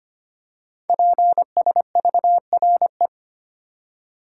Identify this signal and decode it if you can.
Morse code, heard as a keyed tone, about 25 wpm, 700 Hz. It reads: PSVRE
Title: Code Groups - 5 Letter:Numbers - 25wpm